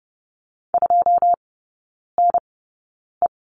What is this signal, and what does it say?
Morse code, heard as a keyed tone, about 30 wpm, spaced as for 9 wpm, 700 Hz. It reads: 2NE
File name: 081 Lesson 14e - Number 2 - Three Characters 30wpm